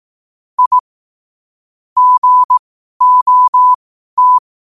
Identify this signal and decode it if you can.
Morse code, heard as a keyed tone, about 18 wpm, spaced as for 12 wpm, 1000 Hz. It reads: I GOT
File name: qotdfiles / 12wpm